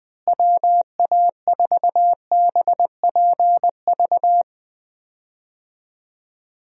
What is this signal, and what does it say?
Morse code, heard as a keyed tone, about 21 wpm, 700 Hz. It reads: WA4BPV
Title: USA Call Signs - 6 Characters - Repeated 3x - 20wpm